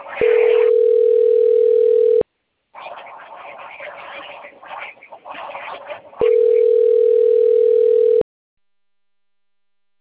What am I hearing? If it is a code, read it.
Ringback tone (North America), 440 + 480 Hz.